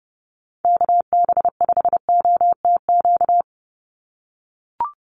Morse code, heard as a keyed tone, about 30 wpm, 700 Hz. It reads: KB5OTQ
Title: USA Call Signs 30wpm